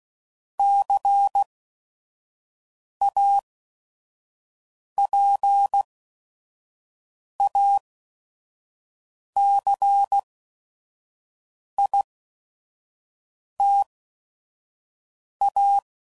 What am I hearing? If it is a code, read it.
Morse code, heard as a keyed tone, about 16 wpm, spaced as for 5 wpm, 800 Hz. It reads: CAPACITA
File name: CW Code Groups @5WPM / Technical Subjects / XC